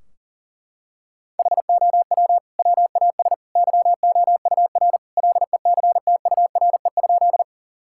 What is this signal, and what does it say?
Morse code, heard as a keyed tone, about 40 wpm, 700 Hz. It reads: HOW WAS YOUR LECTURE?
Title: Sentences from Top 1600 Words 40wpm